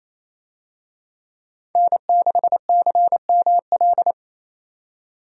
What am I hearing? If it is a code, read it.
Morse code, heard as a keyed tone, about 28 wpm, 700 Hz. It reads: N6CML
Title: USA Call Signs - 5 Characters - 28wpm